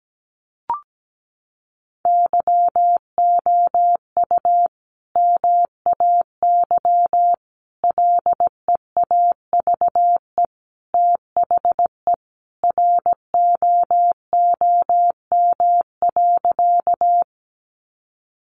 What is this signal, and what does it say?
Morse code, heard as a keyed tone, about 17 wpm, 700 Hz. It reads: YOU MAY LEAVE THE ROOM.
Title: Sentences from Top 400 Words 17wpm